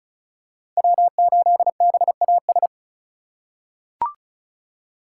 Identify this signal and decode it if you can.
Morse code, heard as a keyed tone, about 34 wpm, 700 Hz. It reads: W8BAS
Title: USA Call Signs - 5 Characters - 3x Character Spacing - 35wpm